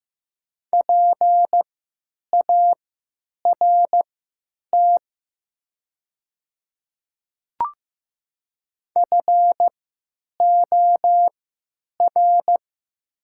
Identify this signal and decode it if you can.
Morse code, heard as a keyed tone, about 15 wpm, spaced as for 8 wpm, 700 Hz. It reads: PART FOR
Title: 3x Character Spacing - Top 500 Words - 15wpm